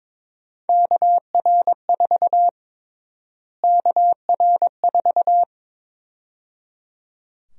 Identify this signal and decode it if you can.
Morse code, heard as a keyed tone, about 22 wpm, 700 Hz. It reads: KR4 KR4
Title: USA Call Sign Prefixes - Repeated 2x - 22wpm